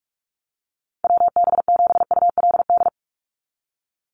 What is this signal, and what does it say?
Morse code, heard as a keyed tone, about 44 wpm, 700 Hz. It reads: WB7ULD
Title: NA SOTA Chaser Callsigns 45wpm